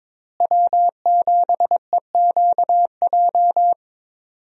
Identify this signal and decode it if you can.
Morse code, heard as a keyed tone, about 22 wpm, 700 Hz. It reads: W7EQJ
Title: USA Call Signs - 5 Characters - Repeated 2x - 22wpm